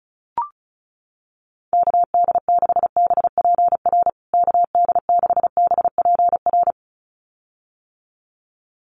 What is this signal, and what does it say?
Morse code, heard as a keyed tone, about 34 wpm, 700 Hz. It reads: KD6BPR KD6BPR
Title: USA Call Signs Repeated 2x 35wpm